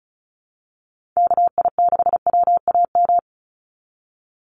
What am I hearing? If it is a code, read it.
Morse code, heard as a keyed tone, about 34 wpm, 700 Hz. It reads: KI6WAM
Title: USA Call Signs 35wpm